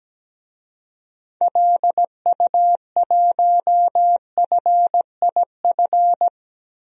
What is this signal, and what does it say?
Morse code, heard as a keyed tone, about 17 wpm, 700 Hz. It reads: LU1FIF
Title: DXCC-100 Call Signs - 6 Characters - Repeated 2x - 17wpm